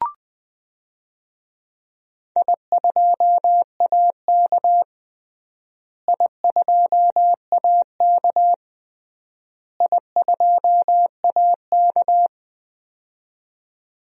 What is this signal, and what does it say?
Morse code, heard as a keyed tone, about 20 wpm, 700 Hz. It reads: I2AK I2AK I2AK
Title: DXCC-100 Call Signs - 4 Characters - Repeated 3x - 20wpm